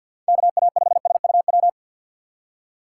Morse code, heard as a keyed tone, about 49 wpm, 700 Hz. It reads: KA5SUW
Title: USA Call Signs - 3x Character Spacing - 50wpm